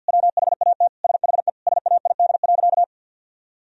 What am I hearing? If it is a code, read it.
Morse code, heard as a keyed tone, about 50 wpm, 700 Hz. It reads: WHAT SHE SAID.
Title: Sentences from Top 1000 Words - Review Part 4 - 50wpm